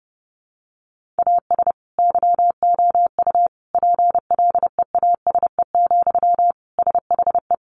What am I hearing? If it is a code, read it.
Morse code, heard as a keyed tone, about 30 wpm, 700 Hz. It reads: AS YOU PLEASE, SHE
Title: Sentences from Top 1500 Words 30wpm